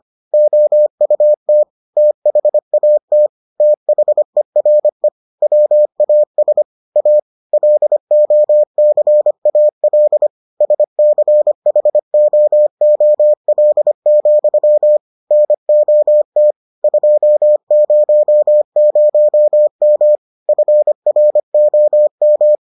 Morse code, heard as a keyed tone, about 25 wpm, 600 Hz. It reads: OUT THAT THERE WAS A LOCAL SCHOOL, NOT 200M FROM